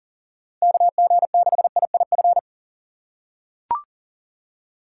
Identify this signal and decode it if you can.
Morse code, heard as a keyed tone, about 40 wpm, 700 Hz. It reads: KG6IIF
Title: USA Call Signs - 2x Character Spacing - 40wpm